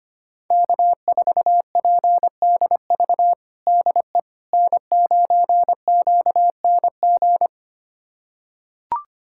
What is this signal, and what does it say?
Morse code, heard as a keyed tone, about 25 wpm, 700 Hz. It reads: K4PDV DE N9QNG
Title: USA Callsign DE Callsign 25wpm